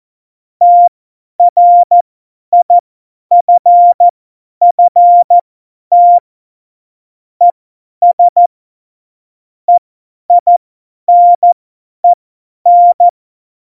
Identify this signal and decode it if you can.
Morse code, heard as a keyed tone, about 14 wpm, spaced as for 10 wpm, 700 Hz. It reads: TRIFFT ES EINEN
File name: Morsekurs / Witze / Witz 15. Dezember 2020, 50 F70